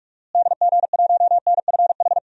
Morse code, heard as a keyed tone, about 46 wpm, 700 Hz. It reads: DG1NFH
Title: DXCC-100 Call Signs - 6 Characters - Repeated 3x - 45wpm